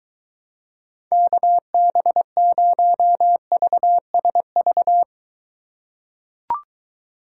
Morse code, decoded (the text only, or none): KB0VSV